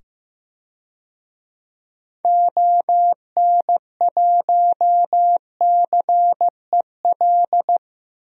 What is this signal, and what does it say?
Morse code, heard as a keyed tone, about 15 wpm, 700 Hz. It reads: ON1CEL
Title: DXCC-100 Call Signs - 6 Characters - 15wpm